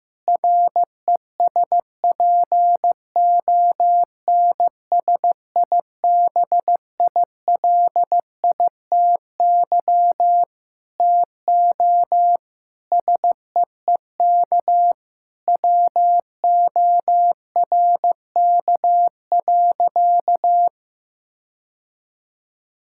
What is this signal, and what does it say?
Morse code, heard as a keyed tone, about 15 wpm, 700 Hz. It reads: RESPONSIBILITY TO SEEK WORK.
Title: Sentences from Top 1600 Words 15wpm